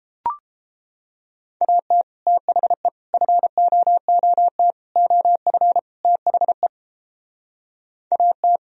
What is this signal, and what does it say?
Morse code, heard as a keyed tone, about 29 wpm, 700 Hz. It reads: AT THE FOOT OF THE AT